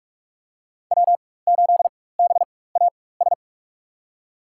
Morse code, heard as a keyed tone, about 46 wpm, spaced as for 21 wpm, 700 Hz. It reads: W8BAS
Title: USA Call Signs - 5 Characters - 4x Character Spacing - 45wpm